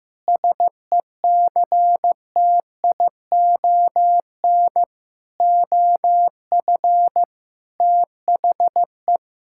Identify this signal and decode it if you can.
Morse code, heard as a keyed tone, about 15 wpm, 700 Hz. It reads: SECTION OF THE
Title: Sentences from Top 1000 Words - Review Part 5 - 15wpm